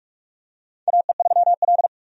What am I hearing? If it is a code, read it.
Morse code, heard as a keyed tone, about 46 wpm, 700 Hz. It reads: AE3L
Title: USA Call Signs - 4 Characters - 45wpm